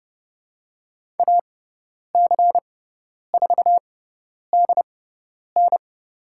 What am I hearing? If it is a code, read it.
Morse code, heard as a keyed tone, about 30 wpm, spaced as for 10 wpm, 700 Hz. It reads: AC4DN
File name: Lesson 17g - Letter u - Call Signs 30wpm